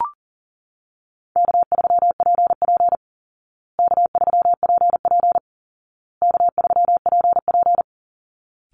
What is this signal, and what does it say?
Morse code, heard as a keyed tone, about 40 wpm, 700 Hz. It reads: K3PP K3PP K3PP
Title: Easy Call Signs Repeated 3x - 40wpm